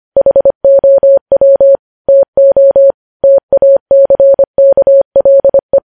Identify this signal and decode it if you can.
Morse code, heard as a keyed tone, about 25 wpm, 550 Hz. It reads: HOW TO TACKLE